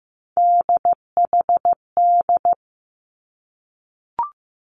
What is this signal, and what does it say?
Morse code, heard as a keyed tone, about 15 wpm, 700 Hz. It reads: DHD
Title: Three Challenging Letter-Number Combinations 15wpm